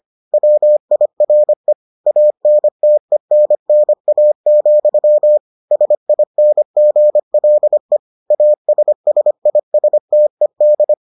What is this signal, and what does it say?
Morse code, heard as a keyed tone, about 25 wpm, 600 Hz. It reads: WIRE ANTENNA, SINGLE ASSISTED